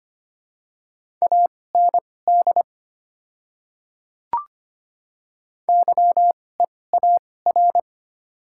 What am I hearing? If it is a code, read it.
Morse code, heard as a keyed tone, about 25 wpm, spaced as for 18 wpm, 700 Hz. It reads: AND YEAR
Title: 2x Character Spacing - Top 200 Words - 25wpm